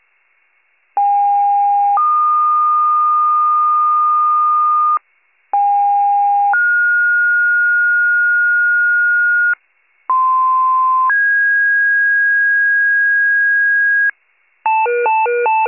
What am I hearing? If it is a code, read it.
Two-tone page, heard as a one-second tone, then a three-second tone, about 810 then 1220 Hz, about 810 then 1430 Hz, about 1040 then 1650 Hz, with noise. It is heard from 1.0 to 14.1 s.